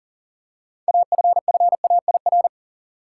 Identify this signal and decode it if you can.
Morse code, heard as a keyed tone, about 40 wpm, 700 Hz. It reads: AFFAIR
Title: Six Letter Words - 40wpm